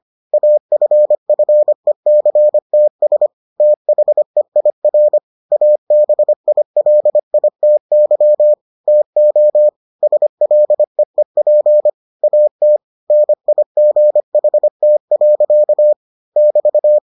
Morse code, heard as a keyed tone, about 25 wpm, 600 Hz. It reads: AFFECTS THEIR ABILITY TO SLEEP AT NIGHT. <BT>